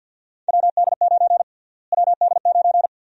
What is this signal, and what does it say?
Morse code, heard as a keyed tone, about 52 wpm, 700 Hz. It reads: WD9 WD9
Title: USA Call Sign Prefixes - Repeated 2x - 50wpm